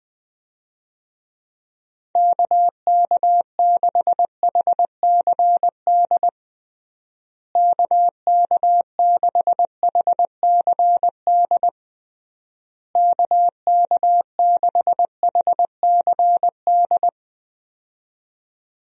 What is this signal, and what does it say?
Morse code, heard as a keyed tone, about 21 wpm, 700 Hz. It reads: KK6HCD KK6HCD KK6HCD
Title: USA Call Signs - 6 Characters - Repeated 3x - 20wpm